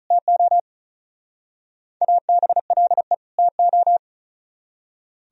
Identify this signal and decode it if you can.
Morse code, heard as a keyed tone, about 44 wpm, 700 Hz. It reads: TO ABLE TO